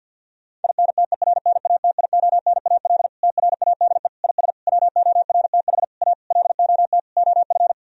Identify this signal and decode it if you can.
Morse code, heard as a keyed tone, about 50 wpm, 700 Hz. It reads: INTERNATIONAL TRADE IS WORTH A LOT OF
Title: Sentences from Top 1000 Words - Review Part 4 - 50wpm